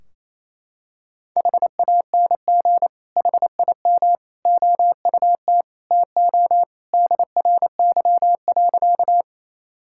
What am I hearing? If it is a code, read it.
Morse code, heard as a keyed tone, about 28 wpm, 700 Hz. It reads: HANG HIM OUT TO DRY.
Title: Sentences from Top 1600 Words 28wpm